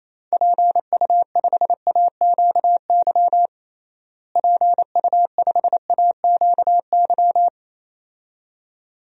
Morse code, heard as a keyed tone, about 28 wpm, 700 Hz. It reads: PU5AQY PU5AQY
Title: DXCC-100 Call Signs - 6 Characters - Repeated 2x - 28wpm